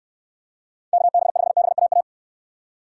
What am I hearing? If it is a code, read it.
Morse code, heard as a keyed tone, about 69 wpm, 700 Hz. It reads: KB5BAA